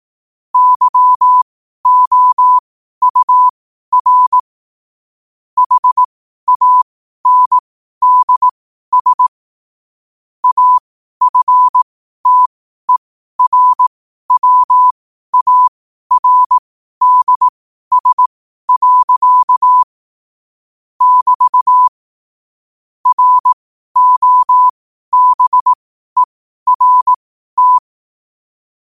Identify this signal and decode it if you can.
Morse code, heard as a keyed tone, about 18 wpm, spaced as for 12 wpm, 1000 Hz. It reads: YOUR HANDS AFTERWARDS. <BT> ROBERT
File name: qotdfiles / 12wpm